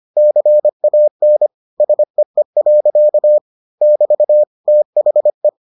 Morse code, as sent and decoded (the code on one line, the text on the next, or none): -.-. .- -. / ... . .-.-.- / -...- / - .... .
CAN SEE. <BT> THE